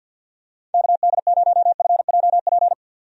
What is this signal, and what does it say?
Morse code, heard as a keyed tone, about 50 wpm, 700 Hz. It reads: KD0FJP
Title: USA Call Signs - 6 Characters - 50wpm